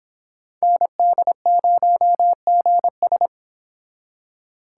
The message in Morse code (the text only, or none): ND0GS